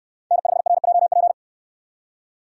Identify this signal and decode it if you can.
Morse code, heard as a keyed tone, about 66 wpm, 700 Hz. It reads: N5RJP